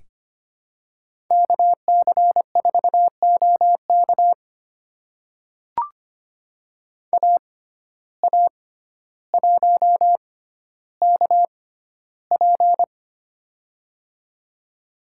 Morse code, heard as a keyed tone, about 25 wpm, spaced as for 13 wpm, 700 Hz. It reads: KC4OK AA1KP
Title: USA Call Signs - 5 Characters - 6x Character Spacing - 25wpm